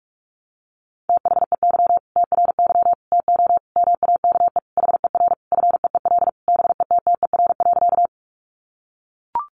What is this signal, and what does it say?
Morse code, heard as a keyed tone, about 44 wpm, 700 Hz. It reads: THEY TRY TO MAKE HER FEEL BETTER.
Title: Easy Phrases 45wpm